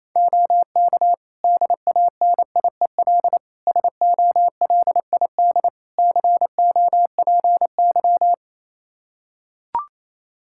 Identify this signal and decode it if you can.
Morse code, heard as a keyed tone, about 28 wpm, 700 Hz. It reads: OK DANIEL SOLID COPY
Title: K1USN SST QSO Practice 28wpm